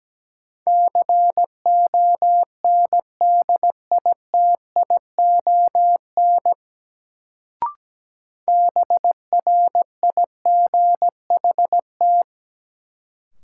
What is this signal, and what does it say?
Morse code, heard as a keyed tone, about 17 wpm, 700 Hz. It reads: CONDITION BRIGHT
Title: Sentences from Top 1000 Words - New Word Drill - 17wpm